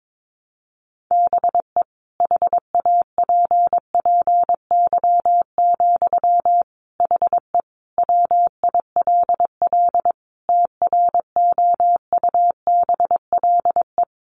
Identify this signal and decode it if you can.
Morse code, heard as a keyed tone, about 22 wpm, 700 Hz. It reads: BE HAPPY, HE WILL TROUBLE
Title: Sentences from Top 1200 Words 22wpm